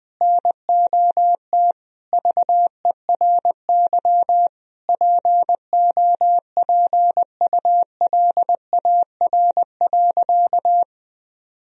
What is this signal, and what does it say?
Morse code, heard as a keyed tone, about 21 wpm, 700 Hz. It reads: NOT VERY POPULAR.
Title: Sentences from Top 1000 Words - Review Part 3 - 20wpm